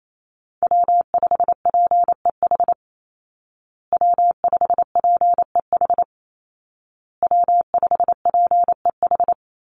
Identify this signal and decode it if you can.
Morse code, heard as a keyed tone, about 28 wpm, 700 Hz. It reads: W5PEH W5PEH W5PEH
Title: Call Signs Repeated 2x - 28wpm